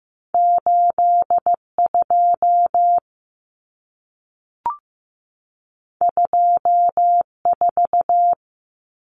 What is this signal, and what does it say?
Morse code, heard as a keyed tone, about 15 wpm, 700 Hz. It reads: 82 24
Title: Common Cut Numbers - 2-Digit 15wpm